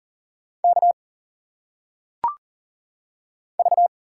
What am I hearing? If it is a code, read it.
Morse code, heard as a keyed tone, about 40 wpm, 700 Hz. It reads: K V
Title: Single Letters - Phonetic Pronunciation 40wpm